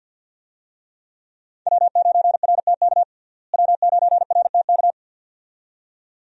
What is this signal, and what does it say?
Morse code, heard as a keyed tone, about 50 wpm, 700 Hz. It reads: W9RTK W9RTK
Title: USA Call Signs - 5 Characters - Repeated 2x - 50wpm